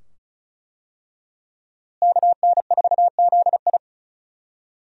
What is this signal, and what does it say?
Morse code, heard as a keyed tone, about 36 wpm, 700 Hz. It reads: KN4ZI
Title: USA Call Signs - 5 Characters - Repeated 2x - 35wpm